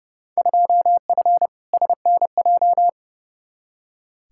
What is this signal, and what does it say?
Morse code, heard as a keyed tone, about 30 wpm, 700 Hz. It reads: 2F SNJ
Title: Field Day Exchanges Encore - 30wpm